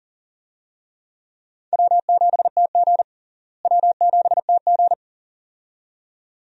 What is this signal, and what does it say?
Morse code, heard as a keyed tone, about 40 wpm, 700 Hz. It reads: W7TG W7TG
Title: USA Call Signs - 4 Characters - Repeated 2x - 40wpm